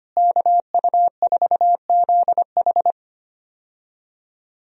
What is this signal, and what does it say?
Morse code, heard as a keyed tone, about 25 wpm, 700 Hz. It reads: KU4ZH
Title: USA Call Signs - 5 Characters - Repeated 3x - 25wpm